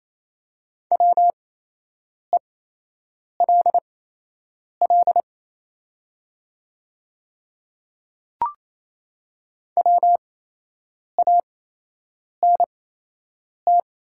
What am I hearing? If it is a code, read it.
Morse code, heard as a keyed tone, about 27 wpm, spaced as for 8 wpm, 700 Hz. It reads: WELL WANT
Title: 8x Character Spacing - Top 200 Words - 28wpm